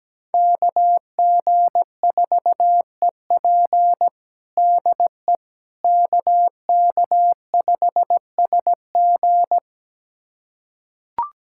Morse code, heard as a keyed tone, about 17 wpm, 700 Hz. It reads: KG4EP DE KK5SG
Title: USA Callsign DE Callsign - 5 Characters - 17wpm